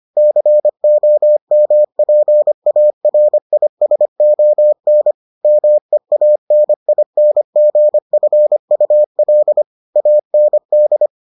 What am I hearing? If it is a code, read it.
Morse code, heard as a keyed tone, about 25 wpm, 600 Hz. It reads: COMPARISON MEANINGFUL AND